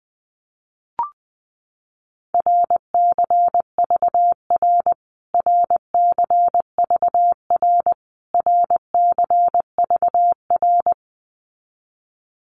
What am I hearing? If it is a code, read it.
Morse code, heard as a keyed tone, about 20 wpm, 700 Hz. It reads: RCVR RCVR RCVR
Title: Abbreviations Repeated 3x - 20wpm